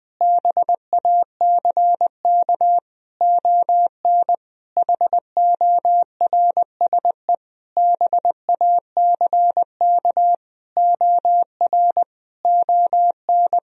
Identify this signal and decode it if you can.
Morse code, heard as a keyed tone, about 20 wpm, 700 Hz. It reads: BACK ON HORSE BACK OR ON
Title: Sentences from Top 1000 Words - Review Part 3 - 20wpm